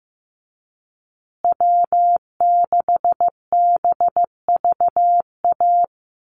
Morse code, heard as a keyed tone, about 15 wpm, 700 Hz. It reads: W6BVA
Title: USA Call Signs Repeated 2x 15wpm